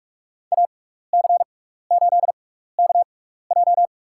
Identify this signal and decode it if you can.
Morse code, heard as a keyed tone, about 44 wpm, spaced as for 15 wpm, 690 Hz. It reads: AC8KJ
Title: USA Call Signs - 5 Characters - 6x Character Spacing - 45wpm